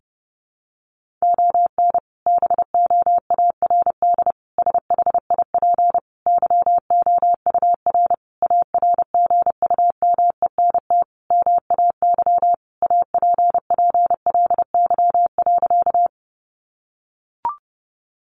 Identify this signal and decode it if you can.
Morse code, heard as a keyed tone, about 30 wpm, 700 Hz. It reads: ON BOARD SHIP YOUR ARGUMENT MAY APPLY.
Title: Sentences from Top 1400 Words 30wpm